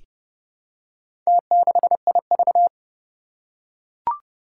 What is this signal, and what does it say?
Morse code, heard as a keyed tone, about 30 wpm, 700 Hz. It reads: T6IV
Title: Four Challenging Letter-Number Combinations 30wpm